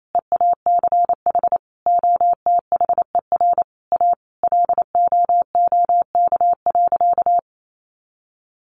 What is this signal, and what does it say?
Morse code, heard as a keyed tone, about 28 wpm, 700 Hz. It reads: EACH OTHER A LOOK.
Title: A Sound Of Thunder 28wpm